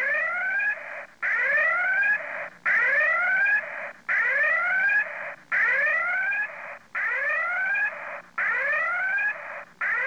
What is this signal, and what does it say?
Emergency Alert System (SAME) tones (header bursts), heard as harsh buzzing bursts. It is heard from 0.9 to 9.8 s.